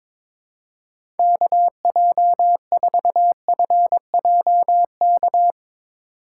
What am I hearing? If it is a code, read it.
Morse code, heard as a keyed tone, about 22 wpm, 700 Hz. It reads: KJ4FJK
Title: USA Call Signs - 4x Character Spacing - 22wpm